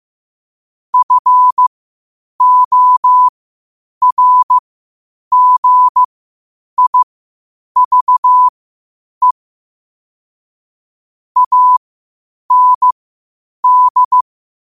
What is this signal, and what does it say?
Morse code, heard as a keyed tone, about 15 wpm, spaced as for 8 wpm, 1000 Hz. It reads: FORGIVE AND
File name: qotdfiles / 8wpm